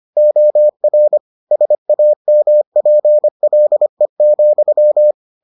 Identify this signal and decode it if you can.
Morse code, heard as a keyed tone, about 25 wpm, 600 Hz. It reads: OR SAMPLE,